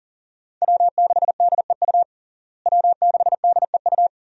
Morse code, heard as a keyed tone, about 40 wpm, 700 Hz. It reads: W6DEU W6DEU
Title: USA Call Signs - 5 Characters - Repeated 3x - 40wpm